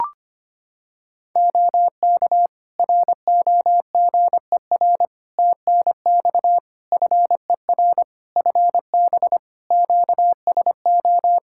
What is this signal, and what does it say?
Morse code, heard as a keyed tone, about 25 wpm, 700 Hz. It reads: OK ROGER TNX FER FB QSO
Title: Practice QSO - Exchange 4 - Ending - 25wpm